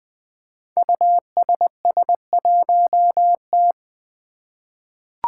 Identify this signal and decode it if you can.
Morse code, heard as a keyed tone, about 20 wpm, 700 Hz. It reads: USS1T
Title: Five Challenging Letter-Number Combinations 20wpm